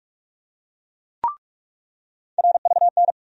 Morse code, heard as a keyed tone, about 46 wpm, 700 Hz. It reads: RVN
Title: USA Call Sign Postfixes - 3 Letters 45wpm